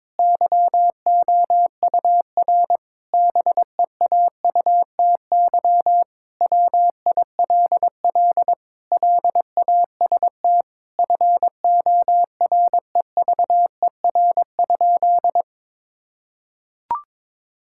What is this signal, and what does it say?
Morse code, heard as a keyed tone, about 22 wpm, 700 Hz. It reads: YOUR BEAUTY WILL LAST FOREVER?
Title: Sentences from Top 1600 Words 22wpm